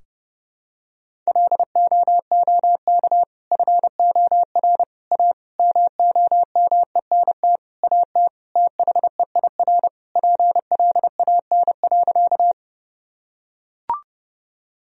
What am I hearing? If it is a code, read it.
Morse code, heard as a keyed tone, about 28 wpm, 700 Hz. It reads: LOOK FOR A MOMENT AT THEIR PLAN.
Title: Sentences from Top 1000 Words - Review Part 2 - 30wpm